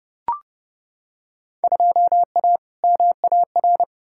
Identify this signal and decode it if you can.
Morse code, heard as a keyed tone, about 31 wpm, 700 Hz. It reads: 2A MAR
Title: Field Day Exchanges - 30wpm